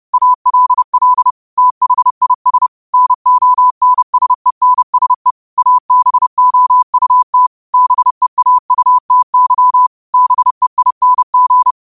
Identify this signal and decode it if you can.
Morse code, heard as a keyed tone, about 30 wpm, 1000 Hz. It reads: ALL THIS NONSENSE ABOUT BEAUTY BEING
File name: qotdfiles / 30wpm